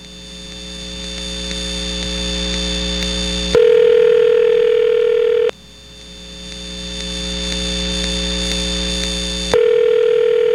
Ringback tone (North America), 440 + 480 Hz.
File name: NYNEX-closed